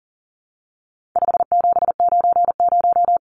Morse code, heard as a keyed tone, about 40 wpm, 700 Hz. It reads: 5790
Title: Numbers - 4-Digit 40wpm